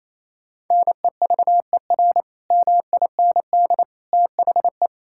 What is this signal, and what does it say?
Morse code, heard as a keyed tone, about 28 wpm, 700 Hz. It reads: NEVER MIND THE
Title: Sentences from Top 1600 Words 28wpm